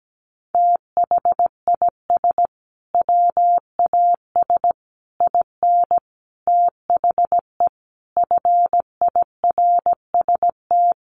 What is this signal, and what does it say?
Morse code, heard as a keyed tone, about 17 wpm, 700 Hz. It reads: THIS WAS IN THE FIRST
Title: Sentences from Top 1300 Words 17wpm